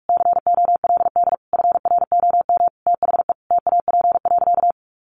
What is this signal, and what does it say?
Morse code, heard as a keyed tone, about 44 wpm, 700 Hz. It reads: COLD FROM THE TAP.
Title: Sentences from Top 1400 Words 45wpm